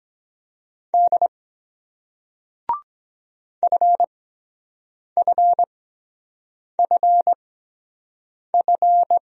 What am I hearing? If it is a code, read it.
Morse code, heard as a keyed tone, about 23 wpm, spaced as for 7 wpm, 700 Hz. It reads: D FFFF